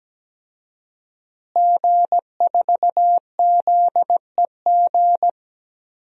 Morse code, heard as a keyed tone, about 17 wpm, 700 Hz. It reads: G4ZEG
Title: DXCC-100 Call Signs - 5 Characters - Repeated 3x - 17wpm